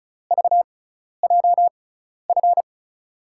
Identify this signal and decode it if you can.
Morse code, heard as a keyed tone, about 36 wpm, spaced as for 12 wpm, 700 Hz. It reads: VJF
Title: USA Call Signs - 6 Characters - 6x Character Spacing - 35wpm